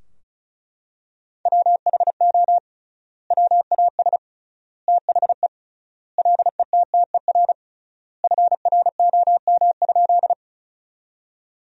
Morse code, heard as a keyed tone, about 36 wpm, 700 Hz. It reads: WHO WAS THE LETTER FROM?
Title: 3x Word-Spacing - Sentences from Top 300 Words - 35wpm